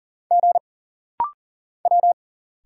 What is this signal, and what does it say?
Morse code, heard as a keyed tone, about 40 wpm, 700 Hz. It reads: G W
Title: Single Letters - Rapid-Fire 40wpm